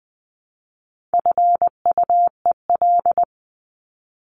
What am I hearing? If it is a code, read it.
Morse code, heard as a keyed tone, about 20 wpm, 700 Hz. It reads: FUEL
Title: Top 1751-2000 Words 20wpm